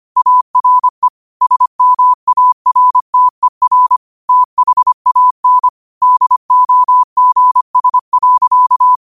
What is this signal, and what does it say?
Morse code, heard as a keyed tone, about 25 wpm, 1000 Hz. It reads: ARE SMARTER THAN DOGS.
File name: qotdfiles / 25wpm